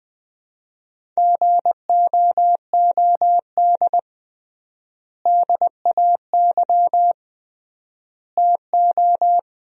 Morse code, heard as a keyed tone, about 21 wpm, 700 Hz. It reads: GOOD DAY TO
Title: 3x Word-Spacing - Sentences from Top 200 Words - 20wpm